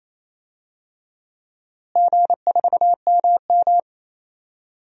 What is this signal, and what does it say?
Morse code, heard as a keyed tone, about 28 wpm, 700 Hz. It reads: G4MM
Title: DXCC-100 Call Signs - 4 Characters - Repeated 2x - 28wpm